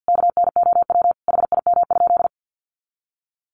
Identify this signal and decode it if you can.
Morse code, heard as a keyed tone, about 50 wpm, 700 Hz. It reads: KNOW HIM?
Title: Sentences from Top 1400 Words 50wpm